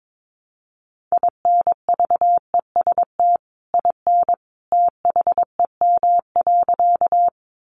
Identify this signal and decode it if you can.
Morse code, heard as a keyed tone, about 22 wpm, 700 Hz. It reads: INVEST IN THEM.
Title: Sentences from Top 1100 Words 22wpm